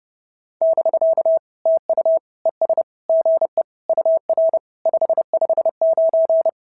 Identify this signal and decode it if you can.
Morse code, heard as a keyed tone, about 30 wpm, 650 Hz. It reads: <BK> TU ES GE UR 559
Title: SOTA QSO Practice 30wpm